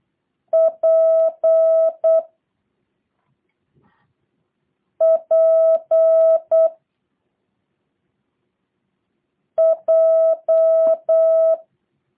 Morse code, heard as a keyed tone, about 8 wpm, 650 Hz. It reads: P P J
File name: jh8cbh / 2023cw / review1-5